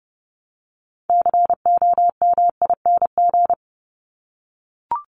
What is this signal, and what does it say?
Morse code, heard as a keyed tone, about 30 wpm, 700 Hz. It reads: COMING
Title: ING Suffix - Top 100 Words with Base Words 30wpm